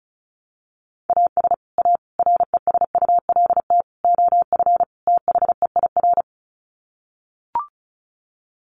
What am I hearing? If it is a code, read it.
Morse code, heard as a keyed tone, about 34 wpm, 700 Hz. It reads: AS A RESULT OF THEIR
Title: Sets of 5 Words Encore2 35wpm